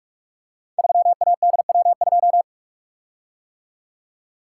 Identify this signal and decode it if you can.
Morse code, heard as a keyed tone, about 44 wpm, 700 Hz. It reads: 3ADW2
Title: Code Groups - 5 Letter:Numbers - 45wpm